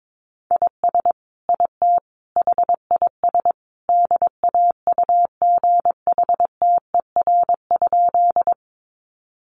Morse code, heard as a keyed tone, about 22 wpm, 700 Hz. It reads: IS IT HIS DAUGHTER?
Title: Sentences from Top 1500 Words 22wpm